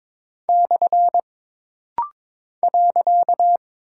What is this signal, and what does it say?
Morse code, heard as a keyed tone, about 22 wpm, 700 Hz. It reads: / .
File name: practice / 22wpm / Single Punctuation-Prosign - Rapid-Fire - 22wpm